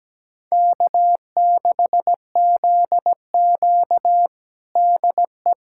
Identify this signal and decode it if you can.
Morse code, heard as a keyed tone, about 17 wpm, 700 Hz. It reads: K6ZQ DE
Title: USA Callsign DE Callsign - 4 Characters - 17wpm